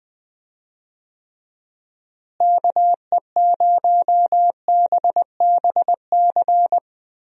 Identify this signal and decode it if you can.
Morse code, heard as a keyed tone, about 21 wpm, 700 Hz. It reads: KE0BBC
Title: USA Call Signs - 6 Characters - Repeated 2x - 20wpm